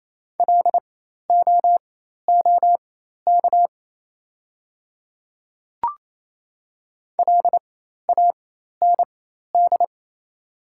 Morse code, heard as a keyed tone, about 27 wpm, spaced as for 13 wpm, 700 Hz. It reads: LOOK LAND
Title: 4x Character Spacing - Top 200 Words - 28wpm